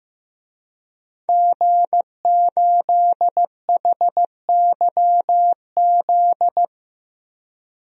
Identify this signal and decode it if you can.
Morse code, heard as a keyed tone, about 15 wpm, 700 Hz. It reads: G8HYZ
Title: DXCC-100 Call Signs - 5 Characters - 15wpm